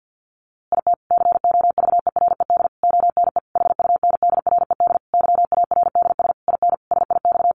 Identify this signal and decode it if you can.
Morse code, heard as a keyed tone, about 49 wpm, 700 Hz. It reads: IT COVERED ONE HUNDRED YARDS IN SIX
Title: A Sound Of Thunder 50wpm